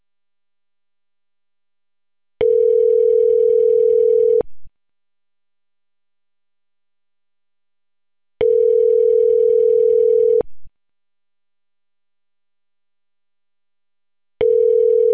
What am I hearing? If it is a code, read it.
Ringback tone (North America), 440 + 480 Hz.